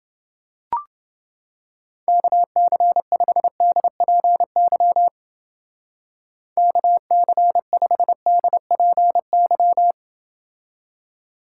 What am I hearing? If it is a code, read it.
Morse code, heard as a keyed tone, about 29 wpm, 700 Hz. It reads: KC5DPY KC5DPY